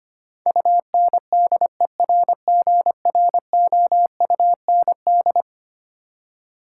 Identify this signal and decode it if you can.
Morse code, heard as a keyed tone, about 25 wpm, 700 Hz. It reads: UNDERGROUND
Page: morse-code-ninja-practice-25wpm